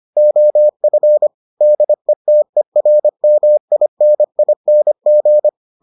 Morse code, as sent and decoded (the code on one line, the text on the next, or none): --- ..-. / -.. - . .-. -- .. -. .. -. --.
OF DETERMINING